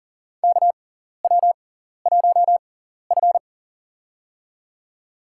Morse code, heard as a keyed tone, about 40 wpm, spaced as for 14 wpm, 700 Hz. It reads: KW1F
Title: USA Call Signs - 4 Characters - 6x Character Spacing - 40wpm